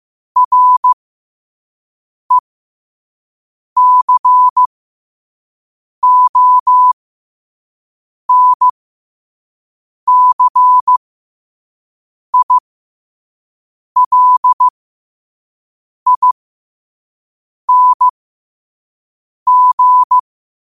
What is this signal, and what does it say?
Morse code, heard as a keyed tone, about 15 wpm, spaced as for 5 wpm, 1000 Hz. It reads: RECONCILING
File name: qotdfiles / 5wpm